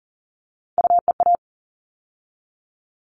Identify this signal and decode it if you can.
Morse code, heard as a keyed tone, about 40 wpm, 700 Hz. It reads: UEA
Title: All Cut Numbers - 3-Digit 40wpm